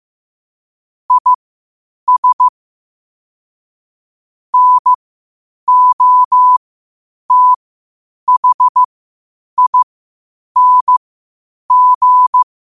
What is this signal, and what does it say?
Morse code, heard as a keyed tone, about 15 wpm, spaced as for 8 wpm, 1000 Hz. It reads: IS NOTHING
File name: qotdfiles / 8wpm